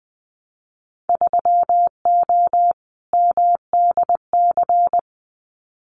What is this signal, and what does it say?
Morse code, heard as a keyed tone, about 20 wpm, 700 Hz. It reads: 3O MDC
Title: Winter Field Day Exchanges Encore 20wpm